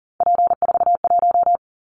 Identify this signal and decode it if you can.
Morse code, heard as a keyed tone, about 40 wpm, 700 Hz. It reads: P41
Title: Callsign Prefixes for Top 100 DXCC - 40wpm